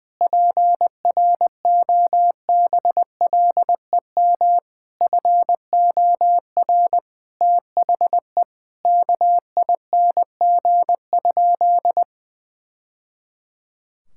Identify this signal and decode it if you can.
Morse code, heard as a keyed tone, about 20 wpm, 700 Hz. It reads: PROBLEM FOR THE KING?
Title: Sentences from Top 1000 Words - Review Part 5 - 20wpm